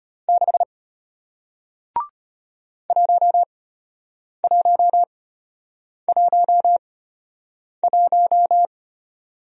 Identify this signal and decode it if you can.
Morse code, heard as a keyed tone, about 33 wpm, spaced as for 8 wpm, 700 Hz. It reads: 6 1111